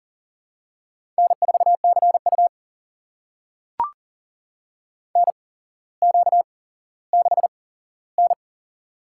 Morse code, heard as a keyed tone, about 40 wpm, spaced as for 17 wpm, 700 Hz. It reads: N4CU NQ6N
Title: USA Call Signs - 4 Characters - 8x Character Spacing - 40wpm